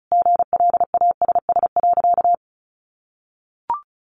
Morse code, heard as a keyed tone, about 34 wpm, 700 Hz. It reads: GLASS.
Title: Sentences from Top 1300 Words 35wpm